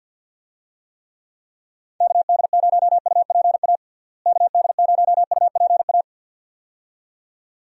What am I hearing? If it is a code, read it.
Morse code, heard as a keyed tone, about 47 wpm, 700 Hz. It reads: KD0UPA KD0UPA
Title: USA Call Signs - 6 Characters - Repeated 2x - 50wpm